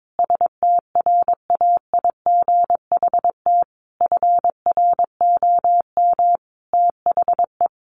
Morse code, heard as a keyed tone, about 22 wpm, 700 Hz. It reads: STRAIGHT FROM THE
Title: Sentences from Top 1500 Words 22wpm